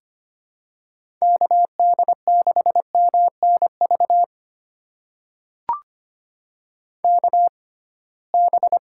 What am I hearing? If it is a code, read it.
Morse code, heard as a keyed tone, about 25 wpm, spaced as for 19 wpm, 700 Hz. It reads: KD6MNV KB
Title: USA Call Signs - 6 Characters - 6x Character Spacing - 25wpm